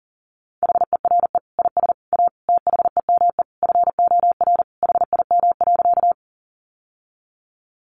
Morse code, heard as a keyed tone, about 40 wpm, 700 Hz. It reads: HERE IS A THEME FOR HIM.
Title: Sentences from Top 1100 Words 40wpm